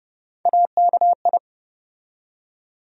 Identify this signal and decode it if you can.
Morse code, heard as a keyed tone, about 30 wpm, 700 Hz. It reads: AKI
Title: USA Call Sign Postfixes - 3 Letters 30wpm